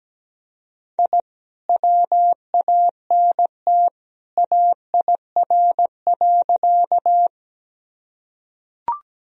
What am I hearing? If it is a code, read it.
Morse code, heard as a keyed tone, about 17 wpm, 700 Hz. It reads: I WANT AIR.
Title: Sentences from Top 1000 Words - Review Part 1 - 17wpm